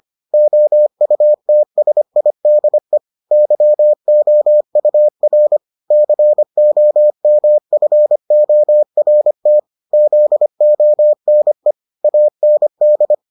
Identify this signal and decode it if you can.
Morse code, heard as a keyed tone, about 25 wpm, 600 Hz. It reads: OUTSIDE YOUR COMFORT ZONE AND